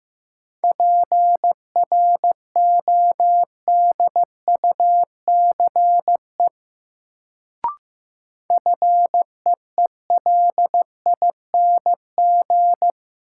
Morse code, heard as a keyed tone, about 16 wpm, 700 Hz. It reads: PRODUCE FEELING
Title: Sentences from Top 1000 Words - Review Part 3 - New Word Drill - 15wpm